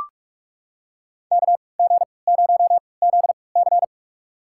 Morse code, heard as a keyed tone, about 43 wpm, spaced as for 25 wpm, 700 Hz. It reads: KG0ZC
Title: USA Call Signs - 5 Characters - 3x Character Spacing - 45wpm